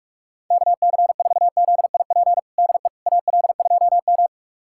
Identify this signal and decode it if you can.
Morse code, heard as a keyed tone, about 46 wpm, 700 Hz. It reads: KC4ZIP DE AL2M
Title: USA Callsign DE Callsign 45wpm